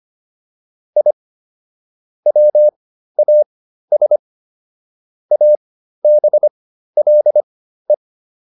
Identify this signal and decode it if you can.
Morse code, heard as a keyed tone, about 25 wpm, spaced as for 13 wpm, 600 Hz. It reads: I WAS ABLE